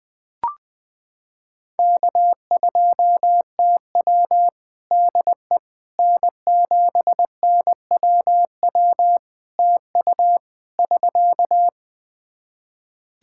Morse code, heard as a keyed tone, about 20 wpm, 700 Hz. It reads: K2TW DE N7NWW TU <SK>
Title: K1USN SST QSO Practice 20wpm